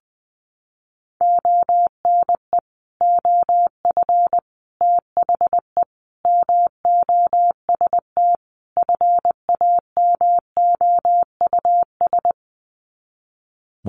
Morse code, heard as a keyed tone, about 20 wpm, 700 Hz. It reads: ONE OF THE MOST FAMOUS
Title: Sets of 5 Words Encore2 20wpm